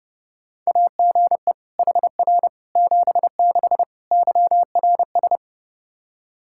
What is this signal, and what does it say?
Morse code, heard as a keyed tone, about 32 wpm, 700 Hz. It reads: AGE HR 76 YRS
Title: Practice QSO - Exchange 3 - More Info - 30wpm